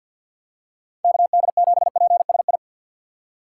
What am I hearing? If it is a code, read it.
Morse code, heard as a keyed tone, about 52 wpm, 700 Hz. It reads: KD7PSI
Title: USA Call Signs - 6 Characters - 3x Character Spacing - 50wpm